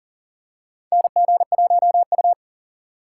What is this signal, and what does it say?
Morse code, heard as a keyed tone, about 40 wpm, 700 Hz. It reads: NG1U
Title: USA Call Signs - 4 Characters - 6x Character Spacing - 40wpm